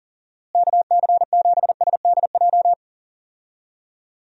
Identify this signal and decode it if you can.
Morse code, heard as a keyed tone, about 40 wpm, 700 Hz. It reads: KC7SDJ
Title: USA Call Signs - 6 Characters - Repeated 3x - 40wpm